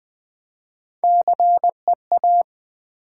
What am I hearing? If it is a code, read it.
Morse code, heard as a keyed tone, about 20 wpm, 700 Hz. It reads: CEA
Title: Build A Word - 1 Letter At A Time - 3 to 5-Letter Words - 20wpm